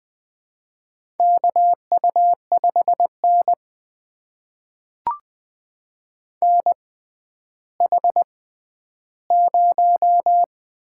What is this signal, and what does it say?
Morse code, heard as a keyed tone, about 20 wpm, spaced as for 11 wpm, 700 Hz. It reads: KU5N NH0
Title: USA Call Signs - 4 Characters - 6x Character Spacing - 20wpm